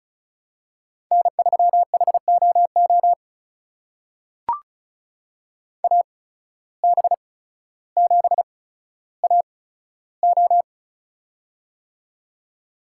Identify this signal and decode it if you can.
Morse code, heard as a keyed tone, about 36 wpm, spaced as for 15 wpm, 700 Hz. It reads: N3HOO AB7AO
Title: USA Call Signs - 5 Characters - 8x Character Spacing - 35wpm